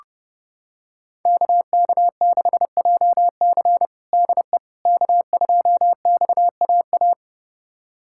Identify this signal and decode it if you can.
Morse code, heard as a keyed tone, about 30 wpm, 700 Hz. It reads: KK6JC DE K2XAA
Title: USA Callsign DE Callsign - 5 Characters - 30wpm